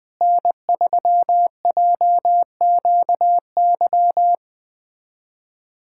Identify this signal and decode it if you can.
Morse code, heard as a keyed tone, about 21 wpm, 700 Hz. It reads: N3JQY
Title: USA Callsign DE Callsign - 5 Characters - 20wpm